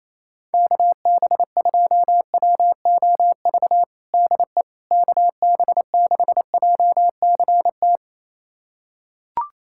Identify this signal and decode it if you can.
Morse code, heard as a keyed tone, about 28 wpm, 700 Hz. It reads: KB2WOV DE KB6JCT
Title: USA Callsign DE Callsign - 6 Characters - 28wpm